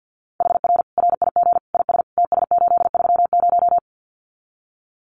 Morse code, heard as a keyed tone, about 50 wpm, 700 Hz. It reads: HR RIG IS TS830
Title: Common QSO Phrases 50wpm